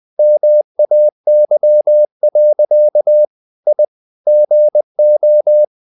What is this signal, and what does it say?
Morse code, heard as a keyed tone, about 20 wpm, 600 Hz. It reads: MAY. I GO
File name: the old man and the sea / the old man and the sea part 1 20wpm